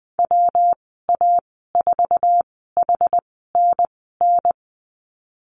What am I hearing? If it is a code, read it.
Morse code, heard as a keyed tone, about 20 wpm, spaced as for 15 wpm, 700 Hz. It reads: WA4HNN
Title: 147 Lesson 24g - Letter y - Call Signs 20wpm